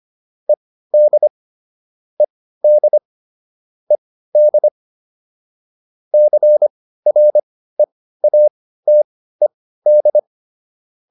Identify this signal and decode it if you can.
Morse code, heard as a keyed tone, about 25 wpm, spaced as for 15 wpm, 600 Hz. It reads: ED ED ED CREATED